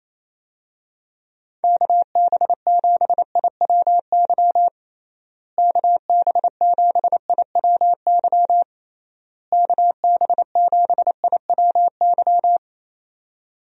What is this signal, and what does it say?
Morse code, heard as a keyed tone, about 27 wpm, 700 Hz. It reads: KB7IWY KB7IWY KB7IWY
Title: USA Call Signs - 6 Characters - Repeated 3x - 28wpm